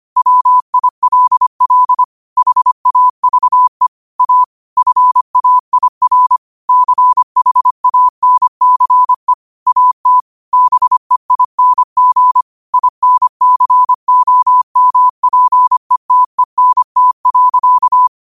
Morse code, heard as a keyed tone, about 25 wpm, 1000 Hz. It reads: WILL HAVE A FAIR CHANCE AT BEING INCOMPETENT.